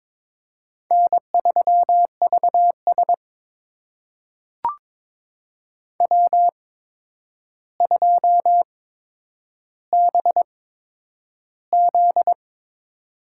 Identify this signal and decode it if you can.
Morse code, heard as a keyed tone, about 22 wpm, spaced as for 9 wpm, 700 Hz. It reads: N3VS W2BZ
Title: USA Call Signs - 4 Characters - 8x Character Spacing - 22wpm